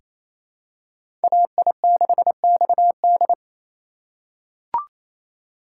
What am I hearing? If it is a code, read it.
Morse code, heard as a keyed tone, about 28 wpm, 700 Hz. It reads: AI6XD
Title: USA Call Signs - 5 Characters - 2x Character Spacing - 28wpm